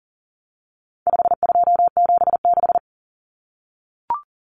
Morse code, heard as a keyed tone, about 40 wpm, 700 Hz. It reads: 5276
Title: Numbers - 4-Digit 40wpm